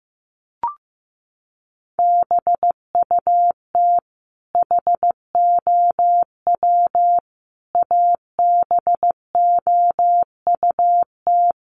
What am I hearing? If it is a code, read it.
Morse code, heard as a keyed tone, about 15 wpm, 700 Hz. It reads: BUT HOW ABOUT
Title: Sentences from Top 1300 Words 15wpm